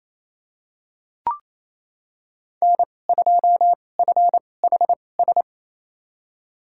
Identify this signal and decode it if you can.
Morse code, heard as a keyed tone, about 28 wpm, spaced as for 20 wpm, 700 Hz. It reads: N2FHS
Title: USA Call Signs - 5 Characters - 2x Character Spacing - 28wpm